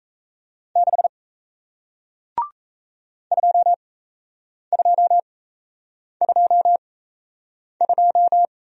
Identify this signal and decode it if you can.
Morse code, heard as a keyed tone, about 38 wpm, spaced as for 8 wpm, 700 Hz. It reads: 6 2222